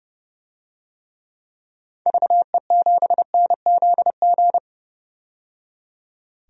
Morse code, heard as a keyed tone, about 32 wpm, 700 Hz. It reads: VE7NZG
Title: DXCC-100 Call Signs - 6 Characters - 30wpm